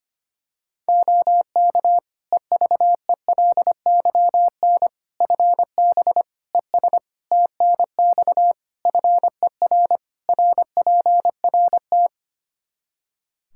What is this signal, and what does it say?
Morse code, heard as a keyed tone, about 25 wpm, 700 Hz. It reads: OK EVELYN FB ES TNX FER RPRT
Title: Practice QSO - Exchange 2 - Info - 25wpm